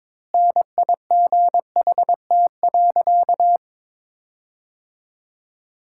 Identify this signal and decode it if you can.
Morse code, heard as a keyed tone, about 22 wpm, 700 Hz. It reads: NIGHT.
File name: Sentences with 7-Word Phrases - 22wpm